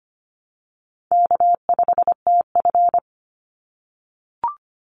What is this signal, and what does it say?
Morse code, heard as a keyed tone, about 25 wpm, 700 Hz. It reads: K5TF
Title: Easy Call Signs - 25wpm